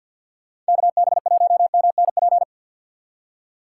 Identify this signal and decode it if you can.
Morse code, heard as a keyed tone, about 66 wpm, spaced as for 57 wpm, 700 Hz. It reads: KB1MNP
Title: Speed Racing - USA 2x3 Call Signs 50wpm